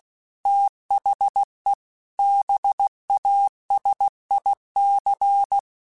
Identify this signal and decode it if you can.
Morse code, heard as a keyed tone, about 16 wpm, 800 Hz. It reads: THE BASIC
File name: Technical Subjects / Technical Subjects-mp3s / Gates Intro @15